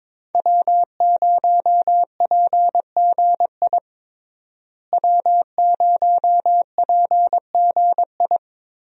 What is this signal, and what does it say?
Morse code, heard as a keyed tone, about 22 wpm, 700 Hz. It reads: W0PGI W0PGI
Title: USA Call Signs - 5 Characters - Repeated 3x - 22wpm